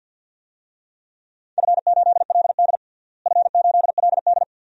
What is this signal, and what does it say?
Morse code, heard as a keyed tone, about 52 wpm, 700 Hz. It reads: F8LD F8LD
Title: DXCC-100 Call Signs - 4 Characters - Repeated 2x - 50wpm